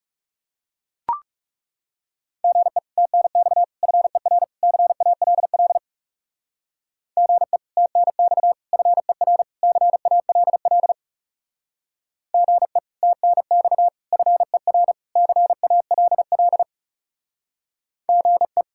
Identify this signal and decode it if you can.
Morse code, heard as a keyed tone, about 36 wpm, 700 Hz. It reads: GE TNX FER CALL GE TNX FER CALL GE TNX FER CALL GE
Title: Speed-Racing Common QSO Phrases - 30wpm